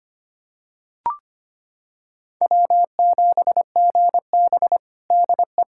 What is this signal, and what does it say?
Morse code, heard as a keyed tone, about 25 wpm, 700 Hz. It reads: W7GB DE
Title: Practice QSO - Exchange 4 - Ending - 25wpm